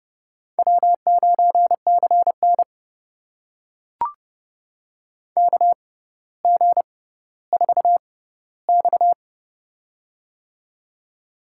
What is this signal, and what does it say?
Morse code, heard as a keyed tone, about 30 wpm, spaced as for 15 wpm, 700 Hz. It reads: W9CN KG4X
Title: USA Call Signs - 4 Characters - 6x Character Spacing - 30wpm